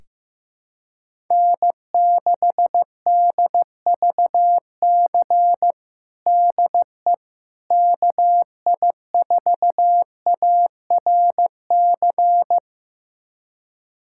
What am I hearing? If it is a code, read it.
Morse code, heard as a keyed tone, about 16 wpm, 700 Hz. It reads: N6DVC DE KI4ARC
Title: USA Callsign DE Callsign 15wpm